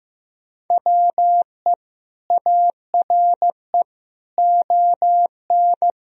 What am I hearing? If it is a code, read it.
Morse code, heard as a keyed tone, about 15 wpm, 700 Hz. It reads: WE ARE ON
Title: Sentences from Top 1000 Words - Review Part 4 - 15wpm